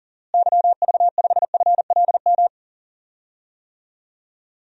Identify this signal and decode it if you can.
Morse code, heard as a keyed tone, about 37 wpm, 700 Hz. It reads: YV5FLM
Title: DXCC-100 Call Signs - 6 Characters - Repeated 3x - 40wpm